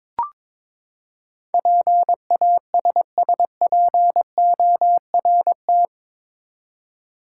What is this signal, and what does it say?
Morse code, heard as a keyed tone, about 22 wpm, 700 Hz. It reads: PASSPORT
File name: practice / 22wpm / Eight Letter Words - 22wpm